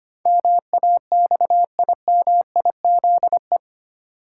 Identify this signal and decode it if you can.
Morse code, heard as a keyed tone, about 25 wpm, 700 Hz. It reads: MAXIMIZE
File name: 8 Letter Words - First 3 Letters Given - 25wpm